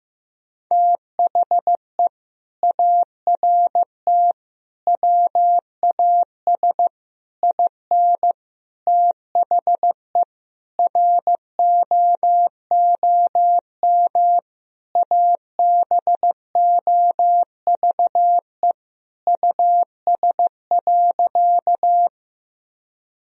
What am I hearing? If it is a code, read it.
Morse code, heard as a keyed tone, about 15 wpm, 700 Hz. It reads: THE ART WAS IN THE ROOM ABOVE US.
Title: Sentences from Top 1000 Words - Review Part 3 - 15wpm